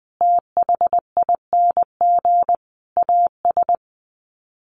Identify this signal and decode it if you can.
Morse code, heard as a keyed tone, about 19 wpm, 700 Hz. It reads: THING AS
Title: Sets of 5 Words Encore 20wpm